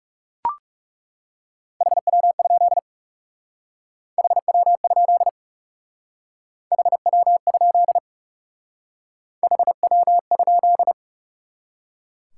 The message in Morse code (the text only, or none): HW? HW? HW? HW?